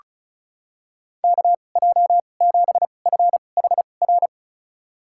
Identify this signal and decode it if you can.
Morse code, heard as a keyed tone, about 36 wpm, spaced as for 25 wpm, 700 Hz. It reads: KJ7FHR
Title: USA Call Signs - 6 Characters - 2x Character Spacing - 35wpm